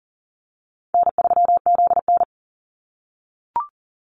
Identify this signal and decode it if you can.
Morse code, heard as a keyed tone, about 40 wpm, 700 Hz. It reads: N3ZN
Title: Easy Call Signs - 40wpm